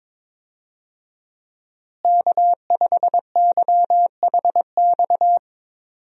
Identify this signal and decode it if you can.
Morse code, heard as a keyed tone, about 22 wpm, 700 Hz. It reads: K5YHX
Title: USA Call Signs - 5 Characters - Repeated 3x - 22wpm